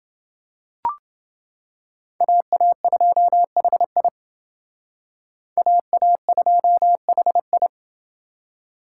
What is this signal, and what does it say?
Morse code, heard as a keyed tone, about 28 wpm, 700 Hz. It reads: AA2HI AA2HI